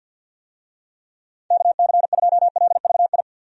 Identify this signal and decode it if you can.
Morse code, heard as a keyed tone, about 52 wpm, 700 Hz. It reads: KC2LVI
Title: USA Call Signs - 6 Characters - 50wpm